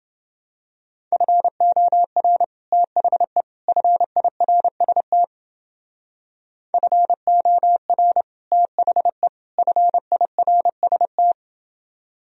Morse code, heard as a keyed tone, about 29 wpm, 700 Hz. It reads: FOR THE FIRST FOR THE FIRST